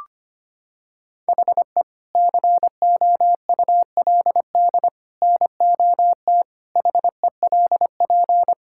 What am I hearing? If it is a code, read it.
Morse code, heard as a keyed tone, about 25 wpm, 700 Hz. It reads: HE COULD NOT HELP
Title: Sentences from Top 1600 Words 25wpm